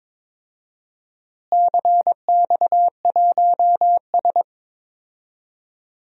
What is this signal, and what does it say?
Morse code, heard as a keyed tone, about 22 wpm, 700 Hz. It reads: CX1S
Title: DXCC-100 Call Signs - 4 Characters - Repeated 3x - 22wpm